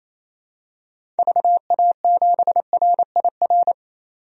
Morse code, heard as a keyed tone, about 27 wpm, 700 Hz. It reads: VA7RIR
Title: DXCC-100 Call Signs - 6 Characters - 28wpm